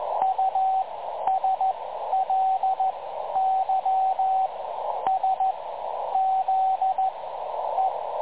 Morse code, heard as a keyed tone, about 14 wpm, spaced as for 10 wpm, 750 Hz, with radio noise. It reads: USLYSZE
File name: dlazaawansowanych / nagraniazpasma / websdr recording 3565.0kHz